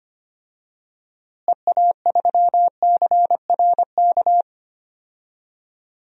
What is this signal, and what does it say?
Morse code, heard as a keyed tone, about 25 wpm, 700 Hz. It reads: EA3CRK
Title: DXCC-100 Call Signs - 6 Characters - Repeated 2x - 25wpm